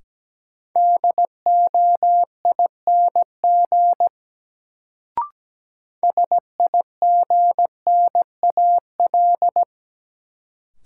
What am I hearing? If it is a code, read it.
Morse code, heard as a keyed tone, about 17 wpm, 700 Hz. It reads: DOING SIGNAL
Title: Sentences from Top 1400 Words - New Word Drill Alt - 17wpm